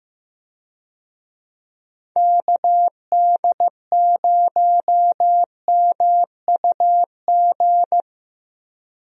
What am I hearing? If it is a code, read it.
Morse code, heard as a keyed tone, about 16 wpm, 700 Hz. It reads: KD0MUG
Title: USA Call Signs - 6 Characters - Repeated 3x - 15wpm